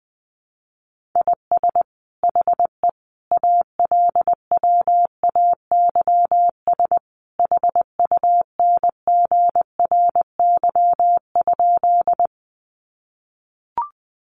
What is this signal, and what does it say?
Morse code, heard as a keyed tone, about 19 wpm, 700 Hz. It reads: IS HE ALWAYS HUNGRY?
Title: Sentences from Top 1400 Words 20wpm